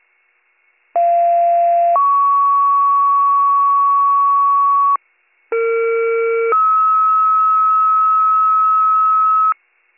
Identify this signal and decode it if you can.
Two-tone page, heard as a one-second tone, then a three-second tone, about 690 then 1120 Hz, about 460 then 1290 Hz, with noise. It is heard from 1.0 to 9.5 s.